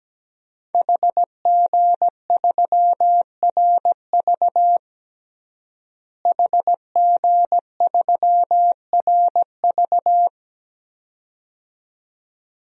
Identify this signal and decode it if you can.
Morse code, heard as a keyed tone, about 17 wpm, 700 Hz. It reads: HG3RV HG3RV
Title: DXCC-100 Call Signs - 5 Characters - Repeated 2x - 17wpm